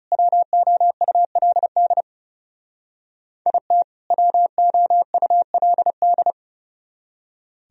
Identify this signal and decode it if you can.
Morse code, heard as a keyed tone, about 39 wpm, 700 Hz. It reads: WOULD IT WOULD